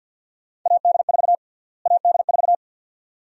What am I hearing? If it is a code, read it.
Morse code, heard as a keyed tone, about 47 wpm, 700 Hz. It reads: AD4 AD4
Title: USA Call Sign Prefixes - Repeated 2x - 50wpm